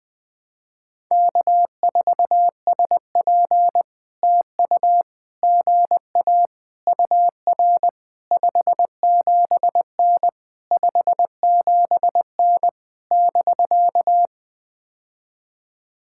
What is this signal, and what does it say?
Morse code, heard as a keyed tone, about 20 wpm, 700 Hz. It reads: K4SP TU GA UR 57N 57N <BK>
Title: POTA QSO Practice - 20wpm